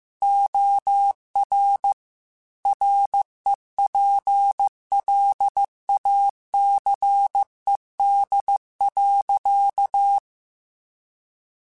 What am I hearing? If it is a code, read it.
Morse code, heard as a keyed tone, about 15 wpm, 800 Hz. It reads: OR REPLACED.
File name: Chuckles / Chuckles-mp3s / Squeaky @13